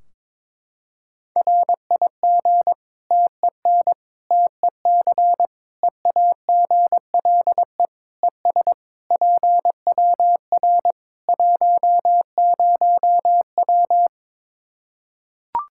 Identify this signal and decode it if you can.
Morse code, heard as a keyed tone, about 22 wpm, 700 Hz. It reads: RIG TEN TEC EAGLE ES PWR 10W
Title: K1USN SST QSO Practice 22wpm